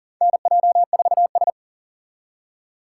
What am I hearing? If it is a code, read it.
Morse code, heard as a keyed tone, about 40 wpm, 700 Hz. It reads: NJ4S
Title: USA Call Signs - 4 Characters - 2x Character Spacing - 40wpm